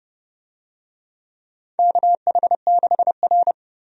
Morse code, heard as a keyed tone, about 32 wpm, 700 Hz. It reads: KH6R
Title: DXCC-100 Call Signs - 4 Characters - Repeated 2x - 30wpm